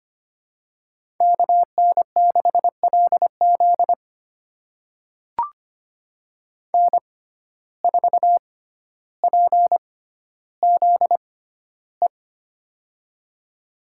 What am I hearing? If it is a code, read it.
Morse code, heard as a keyed tone, about 25 wpm, spaced as for 13 wpm, 700 Hz. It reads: KN6LZ N4PZE
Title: USA Call Signs - 5 Characters - 6x Character Spacing - 25wpm